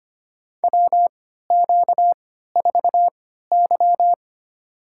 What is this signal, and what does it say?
Morse code, heard as a keyed tone, about 25 wpm, spaced as for 14 wpm, 700 Hz. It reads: WQ4Y
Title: USA Call Signs - 4 Characters - 3x Character Spacing - 25wpm